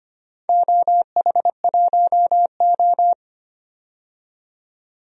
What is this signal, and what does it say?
Morse code, heard as a keyed tone, about 25 wpm, 700 Hz. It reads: OH1O
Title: DXCC-100 Call Signs - 4 Characters - 25wpm